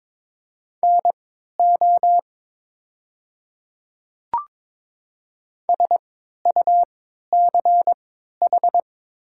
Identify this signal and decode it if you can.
Morse code, heard as a keyed tone, about 22 wpm, spaced as for 13 wpm, 700 Hz. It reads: NO SUCH
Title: 3x Character Spacing - Top 200 Words - 22wpm